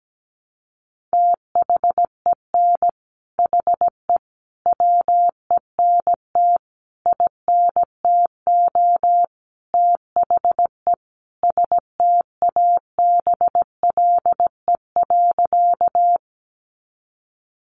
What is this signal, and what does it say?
Morse code, heard as a keyed tone, about 17 wpm, 700 Hz. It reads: THEN HE WENT INTO THE STABLE.
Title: Sentences from Top 1500 Words 17wpm